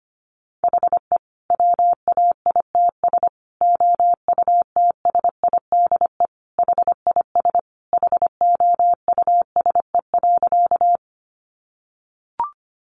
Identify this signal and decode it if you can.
Morse code, heard as a keyed tone, about 25 wpm, 700 Hz. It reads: HE WAITS OUTSIDE HIS HOUSE.
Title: Additional Easy Phrases 25wpm